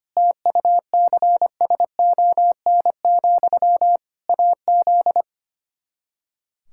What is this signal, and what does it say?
Morse code, heard as a keyed tone, about 25 wpm, 700 Hz. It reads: TUCSON, AZ
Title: Practice QSO - Exchange 1 - Beginning - 25wpm